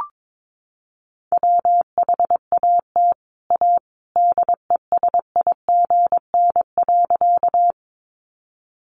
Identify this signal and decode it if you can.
Morse code, heard as a keyed tone, about 22 wpm, 700 Hz. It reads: WHAT A DESIGN.
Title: Sentences from Top 800 Words 22wpm